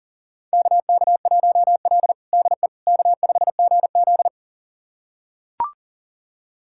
Morse code, heard as a keyed tone, about 37 wpm, 700 Hz. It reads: KK1L DE K5GZ
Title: USA Callsign DE Callsign - 4 Characters - 40wpm